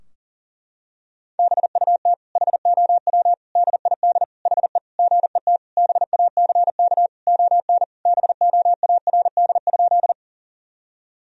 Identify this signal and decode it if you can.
Morse code, heard as a keyed tone, about 40 wpm, 700 Hz. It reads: BUT HOW DID HE GET BACK ON BOARD?
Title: Sentences from Top 1000 Words - Review Part 4 - 40wpm